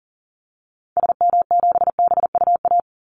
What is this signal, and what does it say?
Morse code, heard as a keyed tone, about 40 wpm, 700 Hz. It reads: SM7BUA
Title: Call Signs Repeated 2x - 40wpm